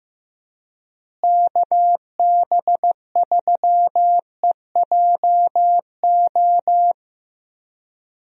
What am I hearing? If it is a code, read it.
Morse code, heard as a keyed tone, about 15 wpm, 700 Hz. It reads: KB3EJO
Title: USA Call Signs - 6 Characters - Repeated 3x - 15wpm